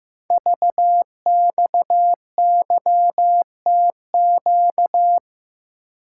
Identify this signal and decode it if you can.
Morse code, heard as a keyed tone, about 15 wpm, 700 Hz. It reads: VXYTQ
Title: Code Groups - 5 Letter:Numbers - 15wpm